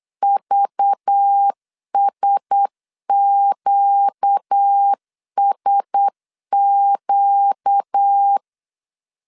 Morse code, heard as a keyed tone, about 8 wpm, 800 Hz. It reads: VSQSQ
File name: CW / DET / 1 6 1500C S B 8wpm 5142